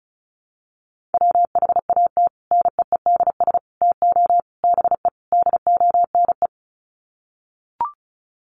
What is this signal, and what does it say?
Morse code, heard as a keyed tone, about 34 wpm, 700 Hz. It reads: WHAT NEEDS TO BE DONE
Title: Sets of 5 Words Encore 35wpm